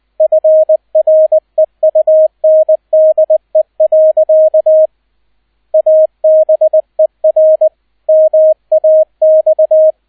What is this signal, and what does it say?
Morse code, heard as a keyed tone, about 20 wpm, 630 Hz, with radio noise. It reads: FREUNDE. ABER MAX